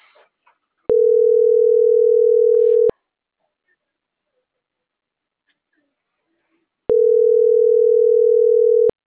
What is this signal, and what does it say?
Ringback tone (North America), 440 + 480 Hz.